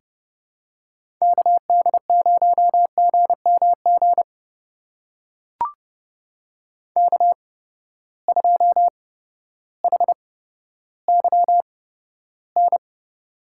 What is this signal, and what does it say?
Morse code, heard as a keyed tone, about 30 wpm, spaced as for 14 wpm, 700 Hz. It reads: KD0GMG K2HYN
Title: USA Call Signs - 8x Character Spacing - 30wpm